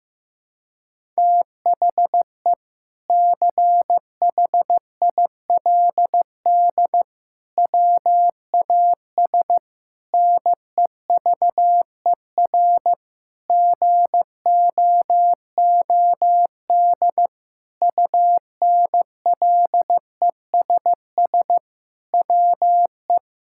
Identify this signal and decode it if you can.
Morse code, heard as a keyed tone, about 15 wpm, 700 Hz. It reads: THE CHILD WAS NEVER GOOD UNLESS WE
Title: Sentences from Top 1000 Words - Review Part 6 - 15wpm